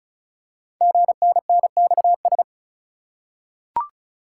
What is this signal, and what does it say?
Morse code, heard as a keyed tone, about 36 wpm, 700 Hz. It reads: GNNXS
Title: Code Groups - 5 Letters - 35wpm